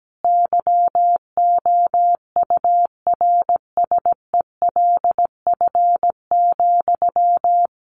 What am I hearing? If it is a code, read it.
Morse code, heard as a keyed tone, about 17 wpm, 700 Hz. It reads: YOURSELF,
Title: Sentences from Top 1300 Words 17wpm